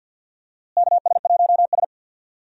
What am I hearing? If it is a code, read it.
Morse code, heard as a keyed tone, about 52 wpm, 700 Hz. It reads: KS1S
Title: USA Call Signs - 4 Characters - Repeated 2x - 50wpm